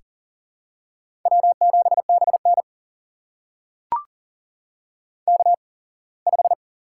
Morse code, heard as a keyed tone, about 39 wpm, spaced as for 24 wpm, 700 Hz. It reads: W7BN K5
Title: USA Call Signs - 8x Character Spacing - 40wpm